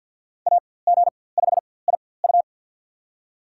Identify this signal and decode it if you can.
Morse code, heard as a keyed tone, about 47 wpm, spaced as for 23 wpm, 700 Hz. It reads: AG5IU
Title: USA Call Signs - 4x Character Spacing - 50wpm